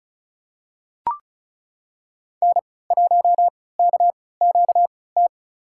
Morse code, heard as a keyed tone, about 36 wpm, spaced as for 20 wpm, 690 Hz. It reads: N1KQT
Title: USA Call Signs - 5 Characters - 3x Character Spacing - 35wpm